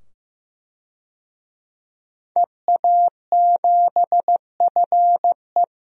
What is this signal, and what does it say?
Morse code, heard as a keyed tone, about 15 wpm, 700 Hz. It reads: EA7FE
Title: DXCC-100 Call Signs - 5 Characters - 15wpm